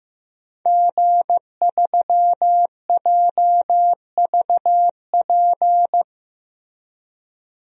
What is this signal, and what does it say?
Morse code, heard as a keyed tone, about 15 wpm, 700 Hz. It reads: G3JVP
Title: DXCC-100 Call Signs - 5 Characters - Repeated 2x - 15wpm